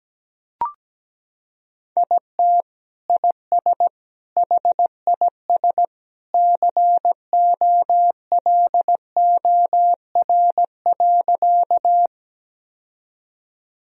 Morse code, heard as a keyed tone, about 17 wpm, 700 Hz. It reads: IT IS HIS COLOR.
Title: Sentences from Top 1000 Words - Review Part 2 - 17wpm